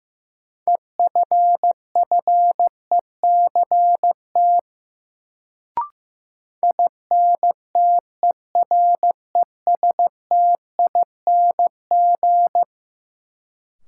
Morse code, heard as a keyed tone, about 15 wpm, 700 Hz. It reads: EFFECT INTERESTING
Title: Sentences from Top 1000 Words - New Word Drill - 15wpm